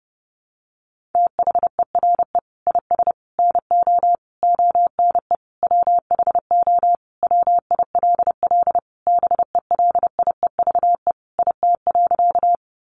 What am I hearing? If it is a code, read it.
Morse code, heard as a keyed tone, about 30 wpm, 700 Hz. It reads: THERE IS NO ONE WHO WILL BELIEVE IT.
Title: Sentences from Top 1000 Words 30wpm